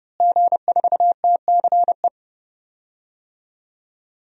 Morse code, heard as a keyed tone, about 30 wpm, 700 Hz. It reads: G4TCE
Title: DXCC-100 Call Signs - 5 Characters - Repeated 3x - 30wpm